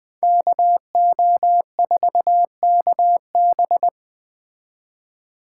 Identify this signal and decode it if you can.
Morse code, heard as a keyed tone, about 21 wpm, 700 Hz. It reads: KO4KB
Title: Speed Racing - USA 2x2 Call Signs 17wpm